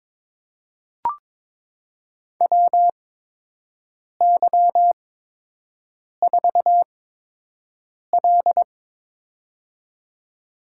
Morse code, heard as a keyed tone, about 23 wpm, spaced as for 6 wpm, 700 Hz. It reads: WY4L
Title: USA Call Signs - 4 Characters - 8x Character Spacing - 22wpm